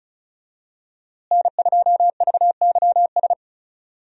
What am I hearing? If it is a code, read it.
Morse code, heard as a keyed tone, about 36 wpm, 700 Hz. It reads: N2VYS
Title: USA Call Signs - 5 Characters - Repeated 2x - 35wpm